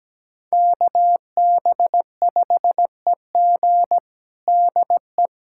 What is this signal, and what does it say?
Morse code, heard as a keyed tone, about 17 wpm, 700 Hz. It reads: KB5EG DE
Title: USA Callsign DE Callsign - 5 Characters - 17wpm